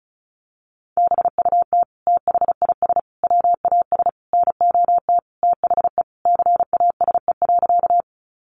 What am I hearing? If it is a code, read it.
Morse code, heard as a keyed tone, about 34 wpm, 700 Hz. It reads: BUT THIS WAS NOT THE CASE.
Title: Sentences from Top 700 Words 35wpm